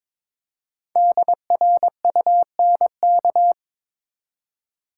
Morse code, heard as a keyed tone, about 22 wpm, 700 Hz. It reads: DRUNK
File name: 5 Letter Words - First 3 Letters Given - 22wpm